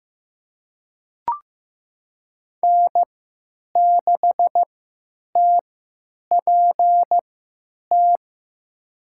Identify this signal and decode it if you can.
Morse code, heard as a keyed tone, about 15 wpm, spaced as for 8 wpm, 700 Hz. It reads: N6TPT
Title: USA Call Signs - 5 Characters - 3x Character Spacing - 15wpm